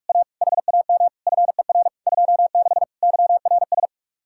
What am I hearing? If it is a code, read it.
Morse code, heard as a keyed tone, about 47 wpm, 700 Hz. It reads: A HAM FER 26 YRS